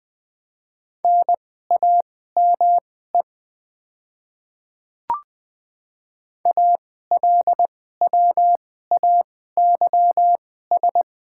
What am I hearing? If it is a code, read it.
Morse code, heard as a keyed tone, about 21 wpm, spaced as for 15 wpm, 700 Hz. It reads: NAME ALWAYS
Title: 2x Character Spacing - Top 500 Words - 20wpm